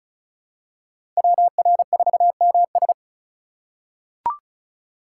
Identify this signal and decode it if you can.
Morse code, heard as a keyed tone, about 36 wpm, 700 Hz. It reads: WR4MS
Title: USA Call Signs - 2x Character Spacing - 35wpm